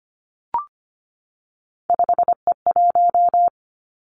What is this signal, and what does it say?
Morse code, heard as a keyed tone, about 25 wpm, 700 Hz. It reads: 5E1
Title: Three Challenging Letter-Number Combinations 25wpm